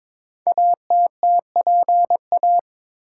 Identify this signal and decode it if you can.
Morse code, heard as a keyed tone, about 22 wpm, 700 Hz. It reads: ATTPA
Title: Code Groups - 5 Letters - 22wpm